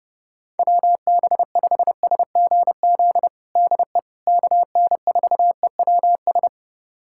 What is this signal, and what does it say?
Morse code, heard as a keyed tone, about 30 wpm, 700 Hz. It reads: WB5SGZ DE KN4EWS
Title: USA Callsign DE Callsign 30wpm